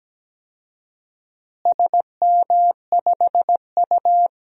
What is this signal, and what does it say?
Morse code, heard as a keyed tone, about 17 wpm, 700 Hz. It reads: SM5U
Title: DXCC-100 Call Signs - 4 Characters - Repeated 3x - 17wpm